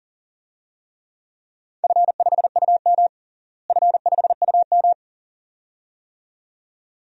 Morse code, heard as a keyed tone, about 40 wpm, 700 Hz. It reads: F5UM F5UM
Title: DXCC-100 Call Signs - 4 Characters - Repeated 2x - 40wpm